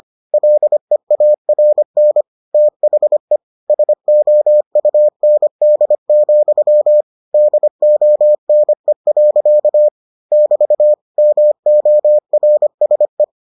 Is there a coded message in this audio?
Yes: Morse code from 0.3 to 13.2 s, heard as a keyed tone, about 25 wpm, 600 Hz. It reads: LEARN THE SOUND, DONE. <BT> MORSE